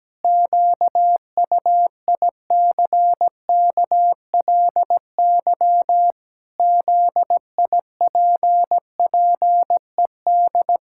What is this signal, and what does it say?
Morse code, heard as a keyed tone, about 17 wpm, 700 Hz. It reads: QUICKLY ZIPPED
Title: Pangram Challenge v2 - 17wpm